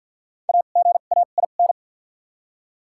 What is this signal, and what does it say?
Morse code, heard as a keyed tone, about 53 wpm, spaced as for 37 wpm, 700 Hz. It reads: AGAIN
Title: 2x Character Spacing - Top 200 Words - 50wpm